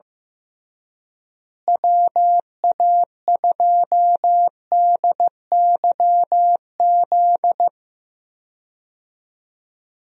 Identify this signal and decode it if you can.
Morse code, heard as a keyed tone, about 15 wpm, 700 Hz. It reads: WA2DYZ
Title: USA Call Signs - 6 Characters - Repeated 3x - 15wpm